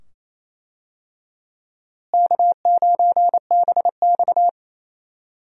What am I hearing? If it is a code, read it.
Morse code, heard as a keyed tone, about 27 wpm, 700 Hz. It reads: K9BX
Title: USA Call Signs - 4 Characters - Repeated 2x - 28wpm